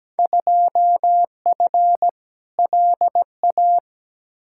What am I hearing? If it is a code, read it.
Morse code, heard as a keyed tone, about 17 wpm, 700 Hz. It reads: 2F LA
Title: Field Day Exchanges - 17wpm